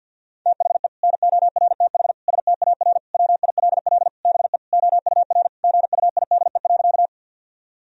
Morse code, heard as a keyed tone, about 50 wpm, 700 Hz. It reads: THE NORTH STAR WILL BE OUR GUIDE.
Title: Sentences from Top 1000 Words - Review Part 3 - 50wpm